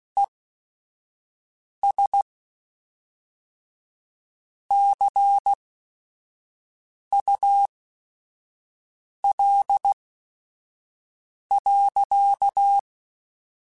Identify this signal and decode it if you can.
Morse code, heard as a keyed tone, about 16 wpm, spaced as for 5 wpm, 800 Hz. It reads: ES CUL.